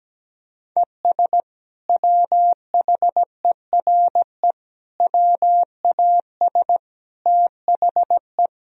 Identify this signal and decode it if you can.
Morse code, heard as a keyed tone, about 17 wpm, 700 Hz. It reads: ES WHERE WAS THE
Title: Sentences from Top 200 Words and Abbreviations 17wpm